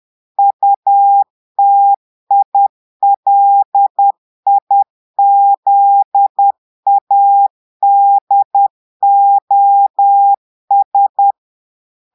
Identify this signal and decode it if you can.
Morse code, heard as a keyed tone, about 10 wpm, 800 Hz. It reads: UTILIZADOS
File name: portugais texte en clair 2 vitesse 10 mots minutes